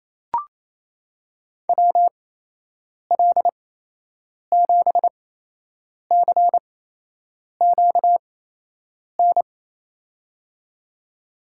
Morse code, heard as a keyed tone, about 28 wpm, spaced as for 8 wpm, 700 Hz. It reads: WL7CQN
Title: USA Call Signs - 6 Characters - 8x Character Spacing - 28wpm